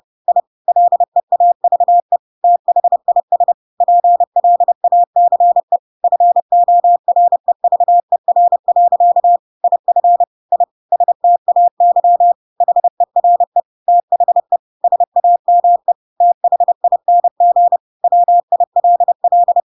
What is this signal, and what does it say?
Morse code, heard as a keyed tone, about 30 wpm, 700 Hz. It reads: I LEAVE THIS PLACE FOREVER. IF I STAY HERE THE SAME THING WILL
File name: Pinocchio / Pinocchio ch4 30 30 ik1hge